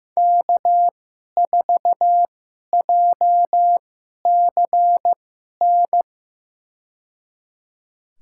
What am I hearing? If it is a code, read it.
Morse code, heard as a keyed tone, about 15 wpm, spaced as for 11 wpm, 700 Hz. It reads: K4JCN
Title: USA Call Signs - 5 Characters - 2x Character Spacing - 15wpm